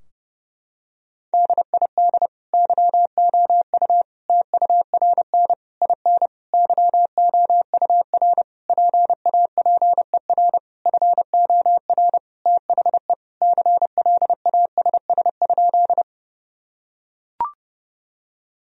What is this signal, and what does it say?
Morse code, heard as a keyed tone, about 30 wpm, 700 Hz. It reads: DID YOU TURN IN YOUR PAPER FOR THE CLASS?
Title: Sentences from Top 1000 Words - Review Part 2 - 30wpm